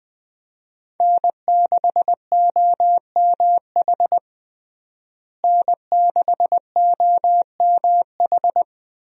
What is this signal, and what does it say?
Morse code, heard as a keyed tone, about 20 wpm, 700 Hz. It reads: N6OMH N6OMH
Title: USA Call Signs - 5 Characters - Repeated 3x - 20wpm